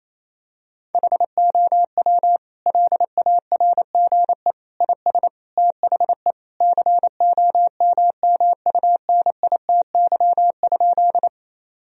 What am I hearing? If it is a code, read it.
Morse code, heard as a keyed tone, about 28 wpm, 700 Hz. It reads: HOW LARGE IS THE COMMUNITY?
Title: Sentences from Top 1000 Words - Review Part 4 - 28wpm